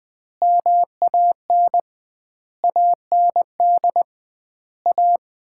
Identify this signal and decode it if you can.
Morse code, heard as a keyed tone, about 20 wpm, 700 Hz. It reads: MAN AND A
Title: 2x Word-Spacing - Sets of 5 Words - 20wpm